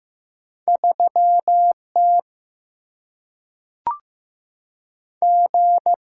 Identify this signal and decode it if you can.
Morse code, heard as a keyed tone, about 15 wpm, 700 Hz. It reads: 3T G
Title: Single-Letter Single-Number 15wpm